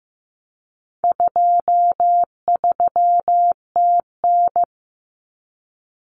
Morse code, heard as a keyed tone, about 15 wpm, 700 Hz. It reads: 23TN
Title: Common Cut Numbers - 4-Digit 15wpm